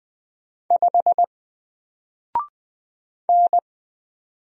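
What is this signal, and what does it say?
Morse code, heard as a keyed tone, about 20 wpm, 700 Hz. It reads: 5 N
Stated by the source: Single letter and number Morse drill at 20 wpm, from N4LCD.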